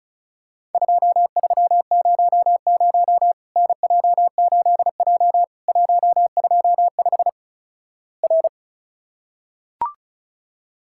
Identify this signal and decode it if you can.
Morse code, heard as a keyed tone, about 36 wpm, 700 Hz. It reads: 2300 NJ8J 125 R
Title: Worked All Europe DX Contest - QTC Exchange - 35wpm